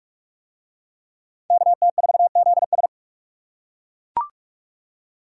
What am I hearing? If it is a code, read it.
Morse code, heard as a keyed tone, about 40 wpm, 700 Hz. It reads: KT4ZS